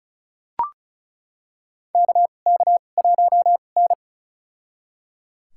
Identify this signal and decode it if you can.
Morse code, heard as a keyed tone, about 36 wpm, spaced as for 25 wpm, 700 Hz. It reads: KK1N
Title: USA Call Signs - 4 Characters - 2x Character Spacing - 35wpm